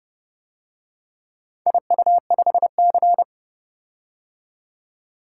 Morse code, heard as a keyed tone, about 30 wpm, 700 Hz. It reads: IU5C